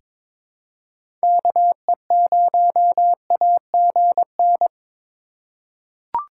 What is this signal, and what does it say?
Morse code, heard as a keyed tone, about 22 wpm, 700 Hz. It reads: KE0AGN
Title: USA Call Signs - 6 Characters - 3x Character Spacing - 22wpm